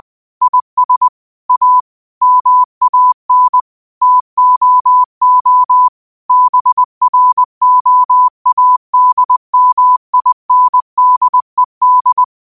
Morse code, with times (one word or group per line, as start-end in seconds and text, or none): 0.4-1.1 IS
1.5-1.8 A
2.2-3.6 MAN
4.0-5.9 TOO
6.3-12.2 BROADMINDED